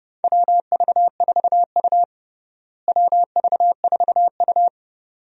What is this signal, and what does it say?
Morse code, heard as a keyed tone, about 30 wpm, 700 Hz. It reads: WV4U WV4U
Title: USA Call Signs - 4 Characters - Repeated 2x - 30wpm